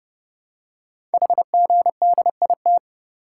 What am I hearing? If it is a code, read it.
Morse code, heard as a keyed tone, about 30 wpm, 700 Hz. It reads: HGDIT
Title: Code Groups - 5 Letters - 30wpm